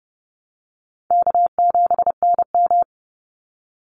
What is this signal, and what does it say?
Morse code, heard as a keyed tone, about 30 wpm, 700 Hz. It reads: K7NM
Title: Call Signs Repeated 3x - 30wpm